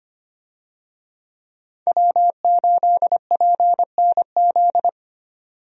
Morse code, heard as a keyed tone, about 25 wpm, 700 Hz. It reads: W8PNZ
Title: USA Call Signs - 5 Characters - 25wpm